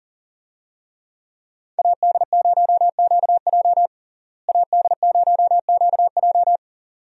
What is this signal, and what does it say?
Morse code, heard as a keyed tone, about 37 wpm, 700 Hz. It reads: AD0QJ AD0QJ
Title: USA Call Signs - 5 Characters - Repeated 2x - 40wpm